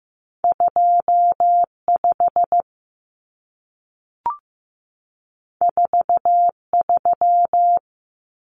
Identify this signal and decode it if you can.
Morse code, heard as a keyed tone, about 15 wpm, 700 Hz. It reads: 25 43
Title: Common Cut Numbers - 2-Digit 15wpm